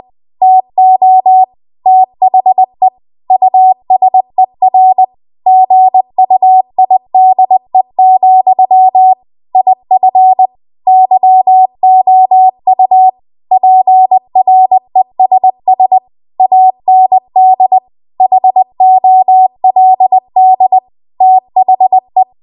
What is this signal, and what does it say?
Morse code, heard as a keyed tone, about 20 wpm, 750 Hz. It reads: TO THE USER GUIDE, IF YOU PRESS AND HOLD THE
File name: Archive / 20 WPM / 20WPM